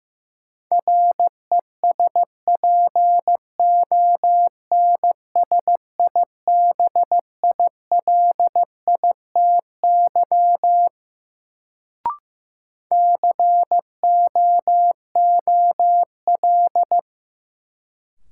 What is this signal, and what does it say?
Morse code, heard as a keyed tone, about 15 wpm, 700 Hz. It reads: RESPONSIBILITY COOL
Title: Sentences from Top 1200 Words - New Word Drill Alt - 15wpm